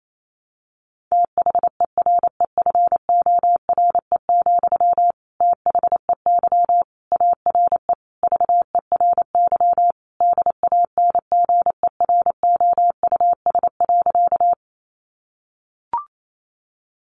Morse code, heard as a keyed tone, about 28 wpm, 700 Hz. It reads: THEREFORE, THEY ARE VERY DANGEROUS.
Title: Sentences from Top 1200 Words 28wpm